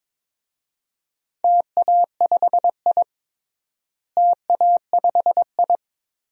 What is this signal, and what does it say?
Morse code, heard as a keyed tone, about 23 wpm, 700 Hz. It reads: TA5I TA5I
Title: DXCC-100 Call Signs - 4 Characters - Repeated 3x - 22wpm